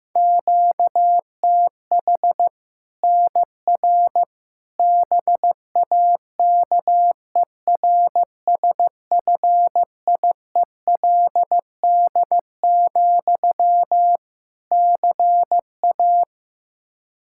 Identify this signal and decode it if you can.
Morse code, heard as a keyed tone, about 15 wpm, 700 Hz. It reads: QTH NR BAKERSFIELD, CA
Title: K1USN SST QSO Practice 15wpm